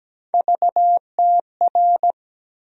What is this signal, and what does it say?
Morse code, heard as a keyed tone, about 17 wpm, 700 Hz. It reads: VTR
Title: USA Call Sign Postfixes - 3 Letters 17wpm